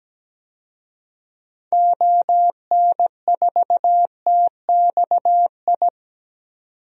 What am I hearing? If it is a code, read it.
Morse code, heard as a keyed tone, about 17 wpm, 700 Hz. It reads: ON4TXI
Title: DXCC-100 Call Signs - 6 Characters - 17wpm